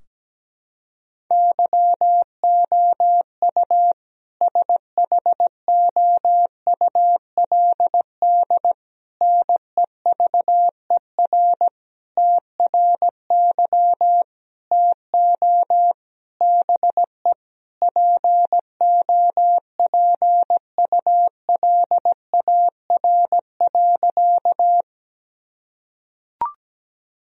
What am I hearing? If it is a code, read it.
Morse code, heard as a keyed tone, about 17 wpm, 700 Hz. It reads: YOU SHOULD NEVER TRY TO BE POPULAR.
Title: Sentences from Top 1000 Words - Review Part 3 - 17wpm